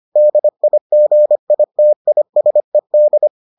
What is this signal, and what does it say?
Morse code, heard as a keyed tone, about 25 wpm, 600 Hz. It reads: DIGITISED